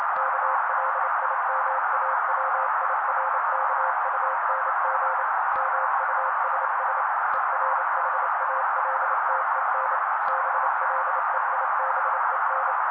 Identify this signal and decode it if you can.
Morse code, heard as a keyed tone, about 28 wpm, 530 Hz. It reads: KLIMAWIRKUNG MUSS ERHALTEN BLEIBEN
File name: daily / 27wpm noise